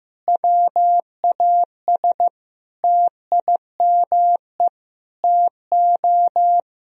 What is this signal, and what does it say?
Morse code, heard as a keyed tone, about 15 wpm, 700 Hz. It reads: WAS TIME TO
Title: Sentences from Top 1000 Words - Review Part 1 - 15wpm